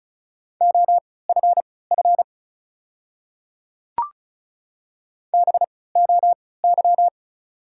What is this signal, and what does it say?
Morse code, heard as a keyed tone, about 35 wpm, spaced as for 20 wpm, 700 Hz. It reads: OFF BOY
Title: 3x Character Spacing - Top 200 Words - 35wpm